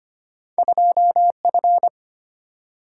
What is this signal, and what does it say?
Morse code, heard as a keyed tone, about 26 wpm, 700 Hz. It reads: 2 F
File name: Two Letter-Number Combinations - 25wpm